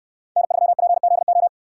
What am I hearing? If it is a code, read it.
Morse code, heard as a keyed tone, about 63 wpm, 700 Hz. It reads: N3PZO